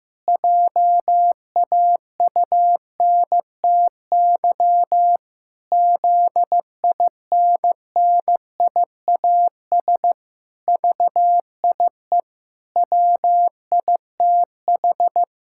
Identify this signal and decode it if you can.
Morse code, heard as a keyed tone, about 15 wpm, 700 Hz. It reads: JAUNTY ZINNIAS VIE WITH
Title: Pangram Challenge v2 - 15wpm